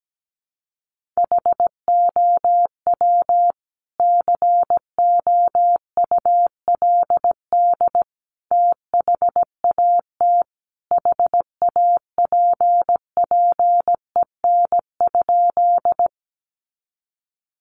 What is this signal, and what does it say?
Morse code, heard as a keyed tone, about 17 wpm, 700 Hz. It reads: HOW COULD THAT HAPPEN?
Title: Sentences from Top 400 Words 17wpm